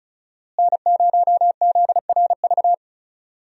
Morse code, heard as a keyed tone, about 36 wpm, 700 Hz. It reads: N0ZRV
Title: USA Call Signs - 5 Characters - Repeated 3x - 35wpm